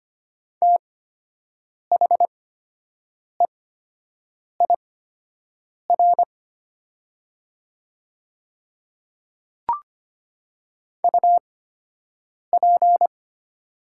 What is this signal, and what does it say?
Morse code, heard as a keyed tone, about 25 wpm, spaced as for 7 wpm, 700 Hz. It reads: THEIR UP
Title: 8x Character Spacing - Top 100 Words - 25wpm